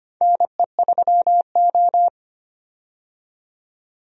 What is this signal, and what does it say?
Morse code, heard as a keyed tone, about 25 wpm, 700 Hz. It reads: NE3O
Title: USA Call Signs - 4 Characters - 25wpm